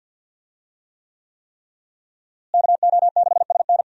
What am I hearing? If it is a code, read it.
Morse code, heard as a keyed tone, about 52 wpm, 700 Hz. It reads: KO6SN
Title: USA Call Signs - 5 Characters - Repeated 3x - 50wpm